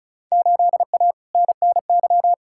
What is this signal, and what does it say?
Morse code, heard as a keyed tone, about 36 wpm, 690 Hz. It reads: 8A NNY
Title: Field Day Exchanges Encore - 35wpm